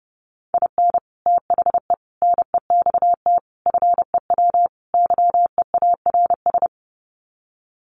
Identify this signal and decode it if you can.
Morse code, heard as a keyed tone, about 30 wpm, 700 Hz. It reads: IN THE NEXT FEW YEARS
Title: Sets of 5 Words Encore 30wpm